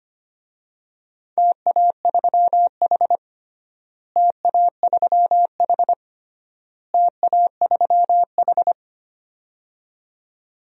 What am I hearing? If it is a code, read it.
Morse code, heard as a keyed tone, about 25 wpm, 700 Hz. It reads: TA3H TA3H TA3H
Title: DXCC-100 Call Signs - 4 Characters - Repeated 3x - 25wpm